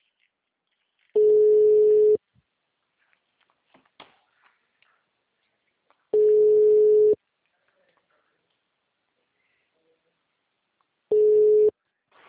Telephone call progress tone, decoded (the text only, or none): ringback tone